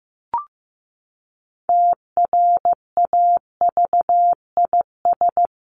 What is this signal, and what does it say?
Morse code, heard as a keyed tone, about 15 wpm, 700 Hz. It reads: TRAVIS
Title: A Sound Of Thunder 15wpm